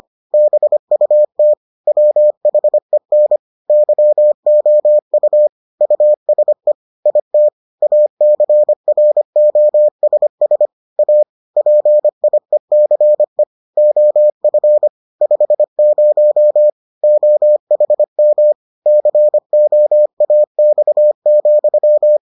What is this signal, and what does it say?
Morse code, heard as a keyed tone, about 25 wpm, 600 Hz. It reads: BUT WHEN YOU USE IT ACROSS A PIECE OF 50 OHM COAX,